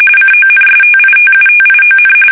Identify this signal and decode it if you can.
RTTY, about 45.45 baud. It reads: CONSTITUTION
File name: fsk850 / usnavy2